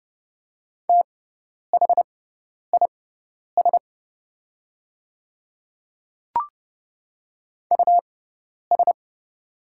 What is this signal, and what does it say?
Morse code, heard as a keyed tone, about 30 wpm, spaced as for 10 wpm, 700 Hz. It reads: THIS US
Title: 6x Character Spacing - Top 200 Words - 30wpm